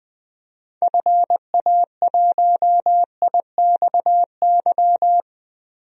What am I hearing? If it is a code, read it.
Morse code, heard as a keyed tone, about 20 wpm, 700 Hz. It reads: FA1IXY
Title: DXCC-100 Call Signs - 6 Characters - Repeated 2x - 20wpm